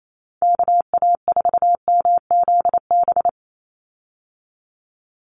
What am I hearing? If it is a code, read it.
Morse code, heard as a keyed tone, about 28 wpm, 700 Hz. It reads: KA4MZB
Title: USA Call Signs 28wpm